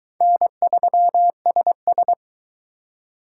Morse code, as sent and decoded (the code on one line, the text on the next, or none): -. ...-- ... ...
N3SS